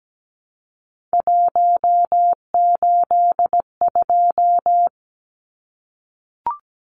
Morse code, heard as a keyed tone, about 17 wpm, 700 Hz. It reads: 182
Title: Numbers - 3-Digit 17wpm